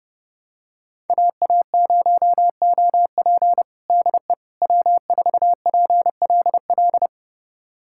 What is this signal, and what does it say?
Morse code, heard as a keyed tone, about 30 wpm, 700 Hz. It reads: AA0OP DE W4PLL
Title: USA Callsign DE Callsign - 5 Characters - 30wpm